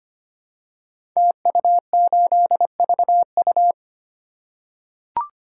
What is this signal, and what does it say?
Morse code, heard as a keyed tone, about 25 wpm, 700 Hz. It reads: TU8VU
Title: Code Groups - 5 Letter:Numbers - 25wpm